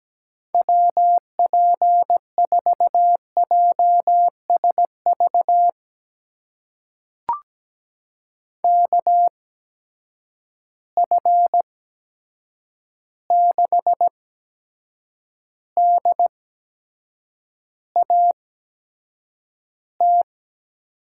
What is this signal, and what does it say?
Morse code, heard as a keyed tone, about 17 wpm, spaced as for 7 wpm, 700 Hz. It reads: WP4JSV KF6DAT
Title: USA Call Signs - 6 Characters - 8x Character Spacing - 17wpm